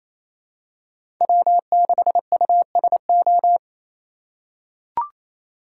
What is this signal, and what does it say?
Morse code, heard as a keyed tone, about 28 wpm, 700 Hz. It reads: W6USO
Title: USA Call Signs - 5 Characters - 2x Character Spacing - 28wpm